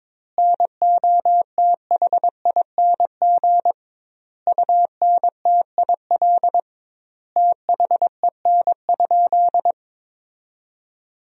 Morse code, heard as a keyed tone, about 22 wpm, 700 Hz. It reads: NOTHING UNTIL THEN?
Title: 2x Word-Spacing - Sentences from Top 500 Words - 22wpm